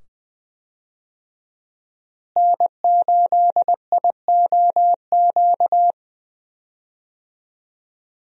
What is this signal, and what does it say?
Morse code, heard as a keyed tone, about 20 wpm, 700 Hz. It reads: N8IOQ
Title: USA Call Signs - 5 Characters - Repeated 2x - 20wpm